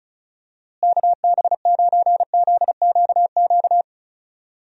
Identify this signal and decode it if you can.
Morse code, heard as a keyed tone, about 36 wpm, 700 Hz. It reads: KB9ZQQ
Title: USA Call Signs - 6 Characters - Repeated 2x - 35wpm